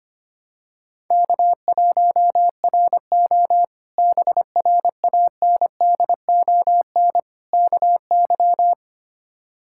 Morse code, heard as a keyed tone, about 25 wpm, 700 Hz. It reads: K1RO BRANDON KY
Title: K1USN SST QSO Practice - 25wpm